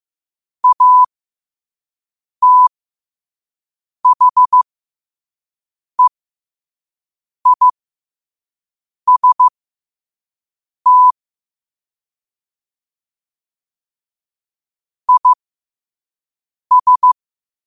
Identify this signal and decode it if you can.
Morse code, heard as a keyed tone, about 15 wpm, spaced as for 5 wpm, 1000 Hz. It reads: ATHEIST IS
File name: qotdfiles / 5wpm